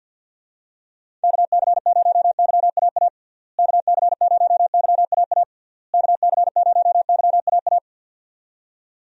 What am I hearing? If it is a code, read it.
Morse code, heard as a keyed tone, about 46 wpm, 700 Hz. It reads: KC0YAA KC0YAA KC0YAA
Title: USA Call Signs - 6 Characters - Repeated 3x - 50wpm